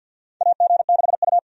Morse code, heard as a keyed tone, about 51 wpm, 700 Hz. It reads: AG6U
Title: USA Call Signs - 4 Characters - 50wpm